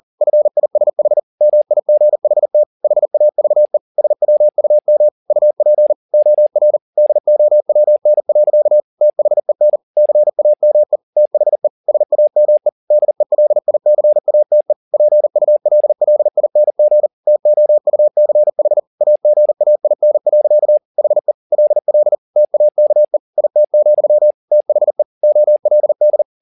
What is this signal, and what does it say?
Morse code, heard as a keyed tone, about 40 wpm, 600 Hz. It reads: FISH MIGHT HAVE SWUM UP OR DOWN. THEN CAME THE SAME DELICATE PULLING TOUCH AGAIN. HE LL TAKE IT, THE OLD